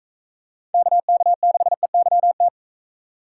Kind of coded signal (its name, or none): Morse code